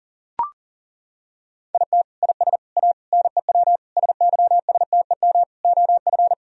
Morse code, heard as a keyed tone, about 37 wpm, 700 Hz. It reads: IT IS A NEW SYSTEM OF
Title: Sentences from Top 1000 Words - Review Part 3 - 40wpm